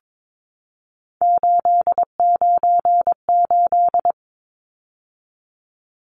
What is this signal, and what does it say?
Morse code, heard as a keyed tone, about 22 wpm, 700 Hz. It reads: 898
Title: Numbers - 3-Digit 22wpm